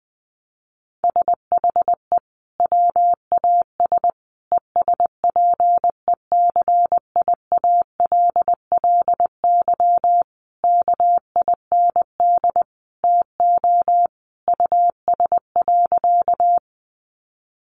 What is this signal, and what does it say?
Morse code, heard as a keyed tone, about 19 wpm, 700 Hz. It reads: SHE WAS ESPECIALLY KIND TO US.
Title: Sentences from Top 700 Words 20wpm